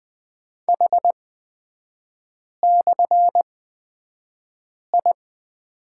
Morse code, heard as a keyed tone, about 20 wpm, spaced as for 5 wpm, 700 Hz. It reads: H/I
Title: 250 Lesson 40e - BK - Three Characters 20wpm